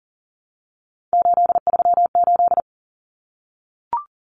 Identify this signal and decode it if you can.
Morse code, heard as a keyed tone, about 40 wpm, 700 Hz. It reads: 838
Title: Numbers - 3-Digit 40wpm